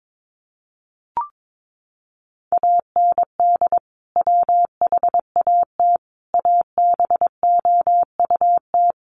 Morse code, heard as a keyed tone, about 22 wpm, 700 Hz. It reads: AND WHAT ABOUT
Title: Sentences from Top 200 Words 22wpm